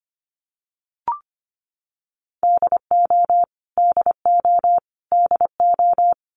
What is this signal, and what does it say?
Morse code, heard as a keyed tone, about 25 wpm, 700 Hz. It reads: DO DO DO
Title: Top 500 Words Repeated 25wpm